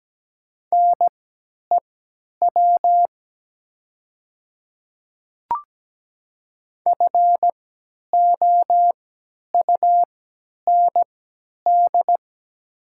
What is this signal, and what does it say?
Morse code, heard as a keyed tone, about 17 wpm, spaced as for 10 wpm, 700 Hz. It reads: NEW FOUND
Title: 3x Character Spacing - Top 500 Words - 17wpm